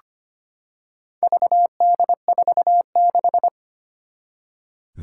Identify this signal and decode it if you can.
Morse code, heard as a keyed tone, about 25 wpm, 700 Hz. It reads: VD46
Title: Four Challenging Letter-Number Combinations 25wpm